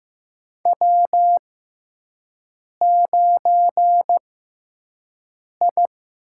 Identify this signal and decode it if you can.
Morse code, heard as a keyed tone, about 15 wpm, spaced as for 5 wpm, 700 Hz. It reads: W9I
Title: USA Call Signs - 6x Character Spacing - 15wpm